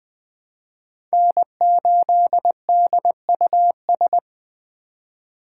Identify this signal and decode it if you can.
Morse code, heard as a keyed tone, about 20 wpm, 700 Hz. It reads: N8DUS
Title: K1USN SST QSO Practice - 20wpm